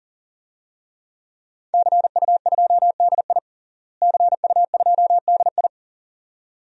Morse code, heard as a keyed tone, about 40 wpm, 700 Hz. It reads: CU2DI CU2DI
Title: DXCC-100 Call Signs - 5 Characters - Repeated 2x - 40wpm